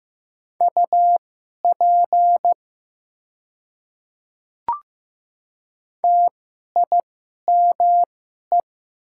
Morse code, heard as a keyed tone, about 15 wpm, spaced as for 11 wpm, 700 Hz. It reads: UP TIME
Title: 2x Character Spacing - Top 200 Words - 15wpm